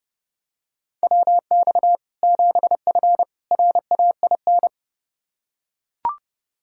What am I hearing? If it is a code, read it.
Morse code, heard as a keyed tone, about 30 wpm, 700 Hz. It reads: WX 7F RAIN
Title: Practice QSO - Exchange 2 - Info - 30wpm